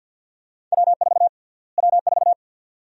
Morse code, heard as a keyed tone, about 50 wpm, 700 Hz. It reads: W4 W4
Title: USA Call Sign Prefixes - Repeated 2x - 50wpm